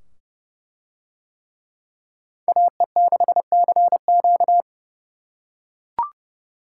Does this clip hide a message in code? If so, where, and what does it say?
Yes: Morse code from 2.5 to 4.6 s, heard as a keyed tone, about 28 wpm, 700 Hz. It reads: AE6CQ